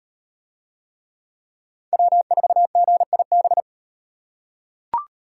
Morse code, heard as a keyed tone, about 38 wpm, 700 Hz. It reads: W4GIB